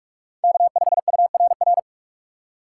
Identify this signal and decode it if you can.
Morse code, heard as a keyed tone, about 48 wpm, 700 Hz. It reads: K5URR
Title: USA Call Signs - 5 Characters - 6x Character Spacing - 45wpm